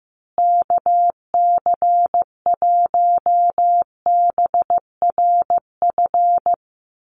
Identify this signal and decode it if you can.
Morse code, heard as a keyed tone, about 15 wpm, 700 Hz. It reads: KC1BRF
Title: USA Call Signs Repeated 2x 15wpm